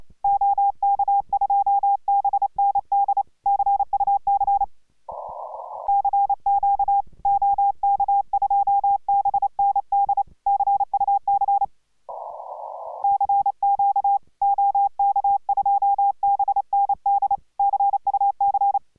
Morse code, heard as a keyed tone, about 28 wpm, 810 Hz. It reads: OK2BND CUC CQ OK2BND CUC CQ OK2BND CUC